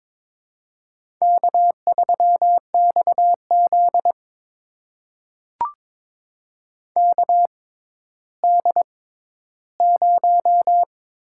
Morse code, heard as a keyed tone, about 22 wpm, spaced as for 12 wpm, 700 Hz. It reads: K3XZ KD0
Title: USA Call Signs - 4 Characters - 6x Character Spacing - 22wpm